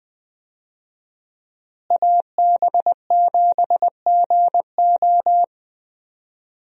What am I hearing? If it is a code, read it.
Morse code, heard as a keyed tone, about 20 wpm, 700 Hz. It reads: AB7GO
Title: USA Call Signs - 5 Characters - Repeated 2x - 20wpm